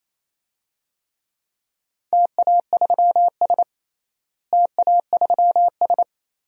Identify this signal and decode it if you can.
Morse code, heard as a keyed tone, about 27 wpm, 700 Hz. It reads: TA3S TA3S
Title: DXCC-100 Call Signs - 4 Characters - Repeated 3x - 28wpm